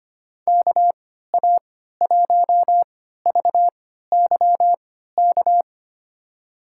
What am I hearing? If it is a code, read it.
Morse code, heard as a keyed tone, about 25 wpm, spaced as for 14 wpm, 700 Hz. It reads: KA1VYK
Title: USA Call Signs - 6 Characters - 3x Character Spacing - 25wpm